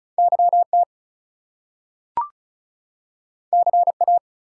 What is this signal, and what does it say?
Morse code, heard as a keyed tone, about 36 wpm, 700 Hz. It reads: YT CA
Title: US States and Canadian Provinces 35wpm